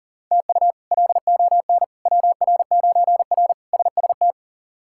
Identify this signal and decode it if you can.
Morse code, heard as a keyed tone, about 40 wpm, 700 Hz. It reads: TU LON WR9R SST
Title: K1USN SST QSO Practice - 40wpm